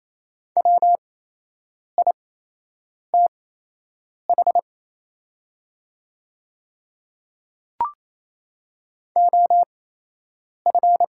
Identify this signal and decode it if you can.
Morse code, heard as a keyed tone, about 28 wpm, spaced as for 8 wpm, 700 Hz. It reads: WITH OF
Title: 8x Character Spacing - Top 200 Words - 28wpm